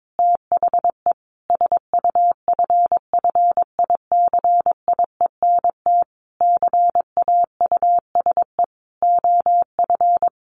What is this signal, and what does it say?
Morse code, heard as a keyed tone, about 22 wpm, 700 Hz. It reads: THE SUFFICIENT CAUSE OF
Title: Sentences from Top 1400 Words 22wpm